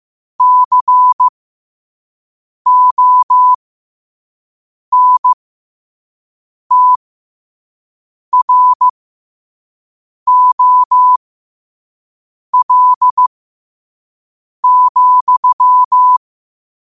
Morse code, heard as a keyed tone, about 15 wpm, spaced as for 5 wpm, 1000 Hz. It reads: CONTROL,
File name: qotdfiles / 5wpm